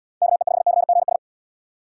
Morse code, heard as a keyed tone, about 72 wpm, 700 Hz. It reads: KE5CGS